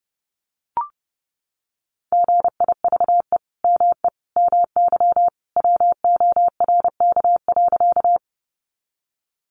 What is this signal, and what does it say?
Morse code, heard as a keyed tone, about 30 wpm, 700 Hz. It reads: GIVE ME MY WORK.
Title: Sentences from Top 200 Words 30wpm